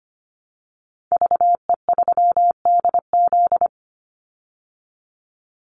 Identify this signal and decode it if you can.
Morse code, heard as a keyed tone, about 25 wpm, 700 Hz. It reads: VE3DZ
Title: Call Signs Repeated 3x - 25wpm